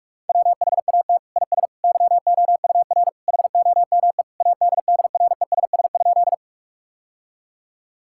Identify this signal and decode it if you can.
Morse code, heard as a keyed tone, about 46 wpm, 700 Hz. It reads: WHAT IS YOUR HOME ADDRESS?
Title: Sentences from Top 1000 Words - Review Part 6 - 45wpm